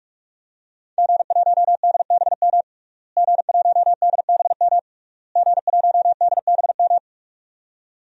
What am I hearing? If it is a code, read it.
Morse code, heard as a keyed tone, about 42 wpm, 700 Hz. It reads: G1DBM G1DBM G1DBM
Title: DXCC-100 Call Signs - 5 Characters - Repeated 3x - 45wpm